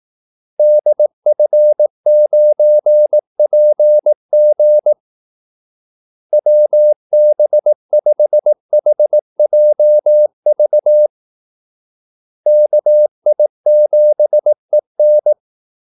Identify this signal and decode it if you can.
Morse code, heard as a keyed tone, about 18 wpm, 600 Hz. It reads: DF9PG WB5HJV KI7EN
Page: Kaldesignaler 18 wpm